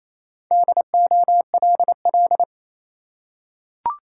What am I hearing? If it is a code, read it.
Morse code, heard as a keyed tone, about 27 wpm, 690 Hz. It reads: DOLL
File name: Easy Words - Speed-Racing with Hints - 22wpm